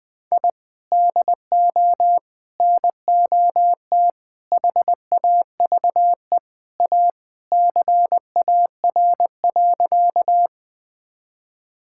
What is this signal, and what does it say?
Morse code, heard as a keyed tone, about 20 wpm, 700 Hz. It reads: I DO NOT HAVE A CAR.
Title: Sentences with 5-Word Phrases - Spoken Beginning - 20wpm